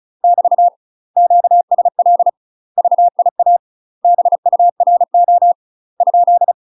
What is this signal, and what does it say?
Morse code, heard as a keyed tone, about 34 wpm, 700 Hz. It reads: <BT> QSL VIA BURO ?
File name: UK MASTER QSOs TRAINING / 35WPM / UK MASTER QSOs 0043